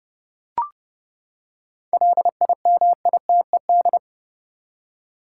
Morse code, heard as a keyed tone, about 30 wpm, 700 Hz. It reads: LIMITED
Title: Seven Letter Words - 30wpm